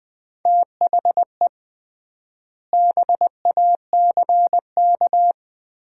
Morse code, heard as a keyed tone, about 20 wpm, 700 Hz. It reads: THE BACK
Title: 3x Word-Spacing - Sets of 3 Words - 20wpm